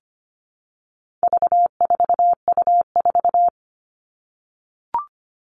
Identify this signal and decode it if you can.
Morse code, heard as a keyed tone, about 25 wpm, 700 Hz. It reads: V4U4
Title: All Cut Numbers - 4-Digit 25wpm